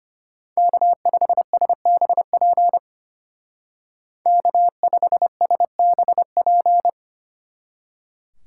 Morse code, heard as a keyed tone, about 31 wpm, 700 Hz. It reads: K5SBP K5SBP
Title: Speed Racing - USA Call Signs 25wpm